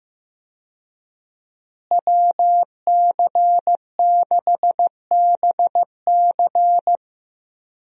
Morse code, heard as a keyed tone, about 15 wpm, 700 Hz. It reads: WC6BC
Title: USA Call Signs - 5 Characters - Repeated 3x - 15wpm